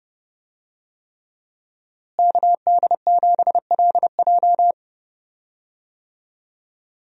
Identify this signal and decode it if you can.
Morse code, heard as a keyed tone, about 29 wpm, 700 Hz. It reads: KD7LJ
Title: USA Call Signs - 5 Characters - Repeated 3x - 30wpm